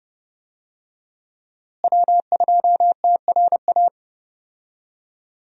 Morse code, heard as a keyed tone, about 31 wpm, 700 Hz. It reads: W2TRA
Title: USA Call Signs - 5 Characters - Repeated 3x - 30wpm